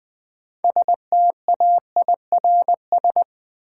Morse code, heard as a keyed tone, about 20 wpm, 700 Hz. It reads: STAIRS
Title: Sets of 5 Words Encore - New Word Drill - 20wpm